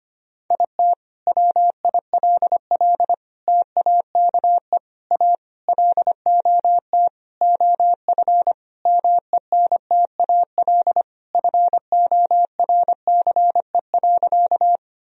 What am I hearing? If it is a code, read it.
Morse code, heard as a keyed tone, about 25 wpm, 700 Hz. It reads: IT WILL TAKE A LOT OF MENTAL FORCE.
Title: Sentences from Top 1000 Words - Review Part 4 - 25wpm